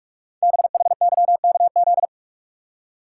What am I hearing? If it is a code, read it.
Morse code, heard as a keyed tone, about 44 wpm, 690 Hz. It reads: BHYKZ
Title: Code Groups - 5 Letters - 45wpm